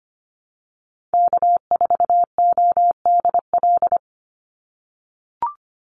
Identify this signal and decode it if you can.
Morse code, heard as a keyed tone, about 25 wpm, 700 Hz. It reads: K4ODL
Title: Call Signs - 25wpm